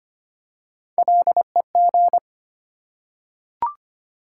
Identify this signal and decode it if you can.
Morse code, heard as a keyed tone, about 25 wpm, 700 Hz. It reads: LEG
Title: Three Letter Words - 25wpm